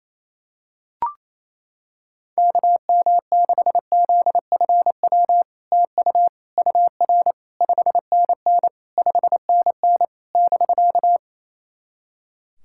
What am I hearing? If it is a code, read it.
Morse code, heard as a keyed tone, about 28 wpm, 700 Hz. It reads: KM6ZFW TU UR 5NN 5NN <BK>
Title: SOTA QSO Practice 28wpm